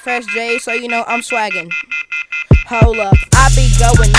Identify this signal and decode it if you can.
Off-hook warning tone, 1400 + 2060 + 2450 + 2600 Hz, with speech.